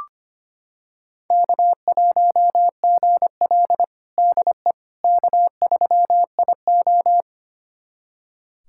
Morse code, heard as a keyed tone, about 25 wpm, 700 Hz. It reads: K1GL DE K3IO
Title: USA Callsign DE Callsign - 4 Characters - 25wpm